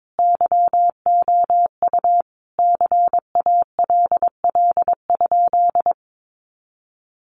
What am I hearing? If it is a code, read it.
Morse code, heard as a keyed tone, about 22 wpm, 700 Hz. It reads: YOU CALL?
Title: Sentences from Top 100 Words 22wpm